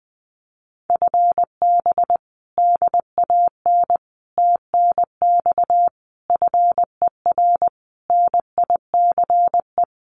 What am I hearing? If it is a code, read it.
Morse code, heard as a keyed tone, about 20 wpm, 700 Hz. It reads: FB DAN TNX FER NICE
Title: Common QSO Phrases Encore 20wpm